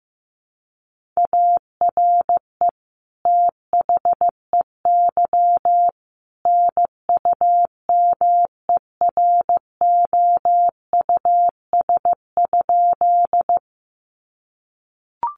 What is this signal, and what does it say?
Morse code, heard as a keyed tone, about 15 wpm, 700 Hz. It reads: ARE THEY NUMEROUS?
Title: Sentences from Top 1400 Words 15wpm